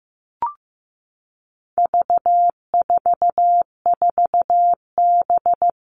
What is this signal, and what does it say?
Morse code, heard as a keyed tone, about 15 wpm, 700 Hz. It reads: V44B
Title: All Cut Numbers - 4-Digit 15wpm